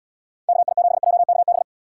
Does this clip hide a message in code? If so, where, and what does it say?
Yes: Morse code from 0.5 to 1.6 s, heard as a keyed tone, about 78 wpm, 700 Hz. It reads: KI7JKB